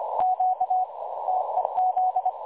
Morse code, heard as a keyed tone, about 25 wpm, 740 Hz. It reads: Q TEZ
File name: dlazaawansowanych / nagraniazpasma / websdr recording 3565.0kHz